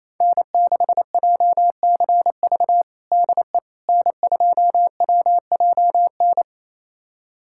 Morse code, heard as a keyed tone, about 28 wpm, 700 Hz. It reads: N6JCV DE N2WJN
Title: USA Callsign DE Callsign - 5 Characters - 28wpm